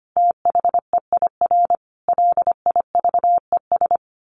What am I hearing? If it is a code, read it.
Morse code, heard as a keyed tone, about 25 wpm, 700 Hz. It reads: THEIR LIVES
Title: Sets of 5 Words Encore 25wpm